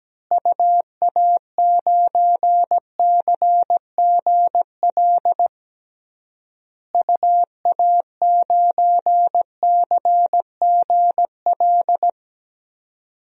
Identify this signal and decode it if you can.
Morse code, heard as a keyed tone, about 17 wpm, 700 Hz. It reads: UA9CGL UA9CGL
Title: DXCC-100 Call Signs - 6 Characters - Repeated 3x - 17wpm